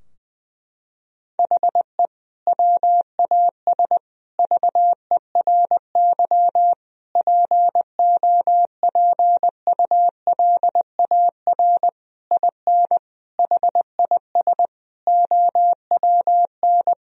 Morse code, heard as a keyed tone, about 20 wpm, 700 Hz. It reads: HE WAS VERY POPULAR IN HIS OWN
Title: Sentences from Top 1000 Words - Review Part 3 - 20wpm